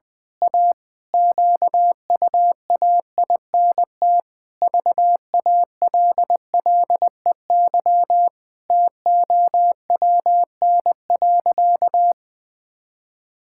Morse code, heard as a keyed tone, about 20 wpm, 700 Hz. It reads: A QUAINT VALLEY TOWN.
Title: Pangram Challenge v2 - 20wpm